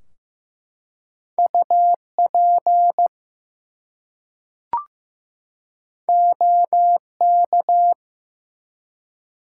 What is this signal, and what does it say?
Morse code, heard as a keyed tone, about 15 wpm, 700 Hz. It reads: UP OK
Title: Two Letter Words - 15wpm